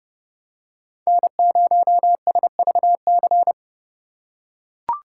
Morse code, heard as a keyed tone, about 28 wpm, 700 Hz. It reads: N0SVC